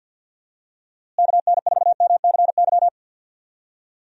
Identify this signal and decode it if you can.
Morse code, heard as a keyed tone, about 48 wpm, 700 Hz. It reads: KN4MCY
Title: USA Call Signs - 6 Characters - Repeated 2x - 50wpm